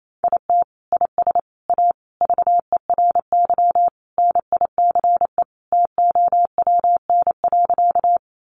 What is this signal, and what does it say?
Morse code, heard as a keyed tone, about 28 wpm, 700 Hz. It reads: IT IS A VERY NICE TOWN.
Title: Sentences from Top 1100 Words 28wpm